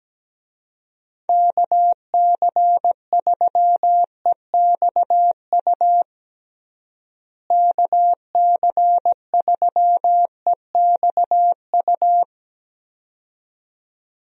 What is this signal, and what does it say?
Morse code, heard as a keyed tone, about 17 wpm, 700 Hz. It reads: KC3EXU KC3EXU
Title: USA Call Signs - 6 Characters - Repeated 2x - 17wpm